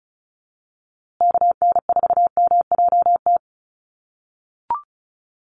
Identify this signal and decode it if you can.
Morse code, heard as a keyed tone, about 34 wpm, 700 Hz. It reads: KN4MJT
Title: USA Call Signs Repeated 3x 35wpm